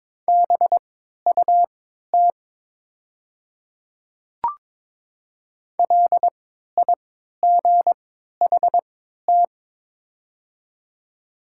Morse code, heard as a keyed tone, about 22 wpm, spaced as for 13 wpm, 700 Hz. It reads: BUT LIGHT
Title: 3x Character Spacing - Top 200 Words - 22wpm